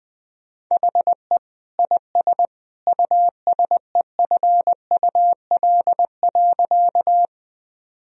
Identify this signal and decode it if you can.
Morse code, heard as a keyed tone, about 20 wpm, 700 Hz. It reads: HE IS USEFUL.
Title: Sentences from Top 1000 Words - Review Part 4 - 20wpm